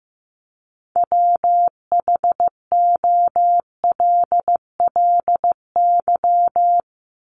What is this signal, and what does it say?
Morse code, heard as a keyed tone, about 15 wpm, 700 Hz. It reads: WHOLLY
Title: LY Suffix - Top 100 Words with Base Words 15wpm